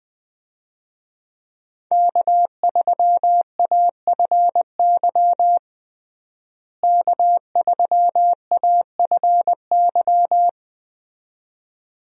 Morse code, heard as a keyed tone, about 20 wpm, 700 Hz. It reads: K3AFY K3AFY
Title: USA Call Signs - 5 Characters - Repeated 2x - 20wpm